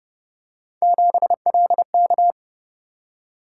Morse code, heard as a keyed tone, about 30 wpm, 700 Hz. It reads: 7LK